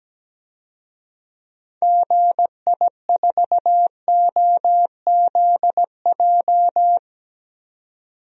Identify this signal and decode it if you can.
Morse code, heard as a keyed tone, about 17 wpm, 700 Hz. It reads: GI4OZJ
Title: DXCC-100 Call Signs - 6 Characters - 17wpm